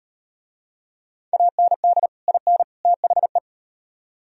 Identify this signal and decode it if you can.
Morse code, heard as a keyed tone, about 38 wpm, 700 Hz. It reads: AND IN THE